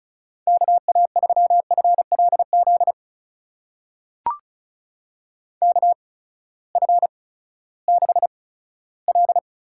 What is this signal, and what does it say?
Morse code, heard as a keyed tone, about 36 wpm, spaced as for 18 wpm, 700 Hz. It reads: KA3FLZ KF6L
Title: USA Call Signs - 8x Character Spacing - 35wpm